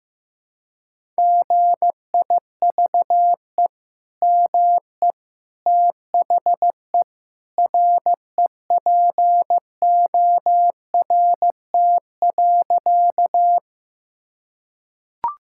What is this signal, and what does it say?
Morse code, heard as a keyed tone, about 15 wpm, 700 Hz. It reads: GIVE ME THE REPORT.
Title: Sentences from Top 1000 Words - Review Part 6 - 15wpm